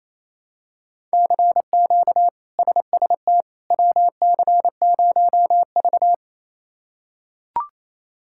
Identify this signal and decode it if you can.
Morse code, heard as a keyed tone, about 28 wpm, 700 Hz. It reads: CQ SST WC0V
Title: K1USN SST QSO Practice - 28wpm